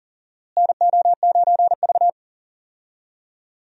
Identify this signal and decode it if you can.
Morse code, heard as a keyed tone, about 40 wpm, 700 Hz. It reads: NO9V
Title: USA Call Signs - 4 Characters - Repeated 2x - 40wpm